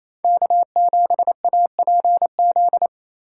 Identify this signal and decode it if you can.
Morse code, heard as a keyed tone, about 37 wpm, 700 Hz. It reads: K7APZ